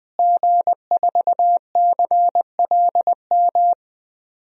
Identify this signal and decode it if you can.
Morse code, heard as a keyed tone, about 21 wpm, 700 Hz. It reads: G4CLM
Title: DXCC-100 Call Signs - 5 Characters - Repeated 3x - 20wpm